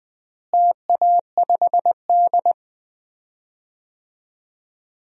Morse code, heard as a keyed tone, about 20 wpm, 700 Hz. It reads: TA5D
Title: DXCC-100 Call Signs - 4 Characters - Repeated 2x - 20wpm